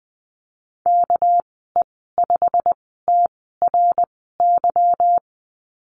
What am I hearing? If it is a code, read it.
Morse code, heard as a keyed tone, about 19 wpm, spaced as for 14 wpm, 700 Hz. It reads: KE5TRY
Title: 245 Lesson 39g - Letter z - Call Signs 20wpm